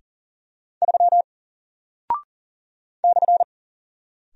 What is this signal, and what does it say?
Morse code, heard as a keyed tone, about 41 wpm, 700 Hz. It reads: 3 /
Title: Single Character (Letter:Number:Punctuation:Prosign) 40wpm